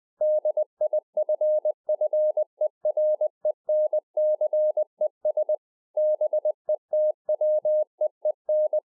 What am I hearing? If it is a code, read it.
Morse code, heard as a keyed tone, about 20 wpm, 610 Hz. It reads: DIFFERENCES BETWEEN